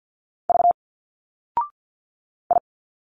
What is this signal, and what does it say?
Morse code, heard as a keyed tone, about 50 wpm, 700 Hz. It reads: V I
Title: Single Letters 50wpm